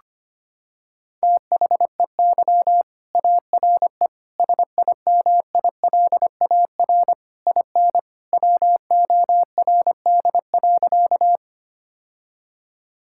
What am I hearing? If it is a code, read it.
Morse code, heard as a keyed tone, about 25 wpm, 700 Hz. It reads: THEY ARE SIMILAR IN WORD.
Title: Sentences from Top 1000 Words - Review Part 5 - 25wpm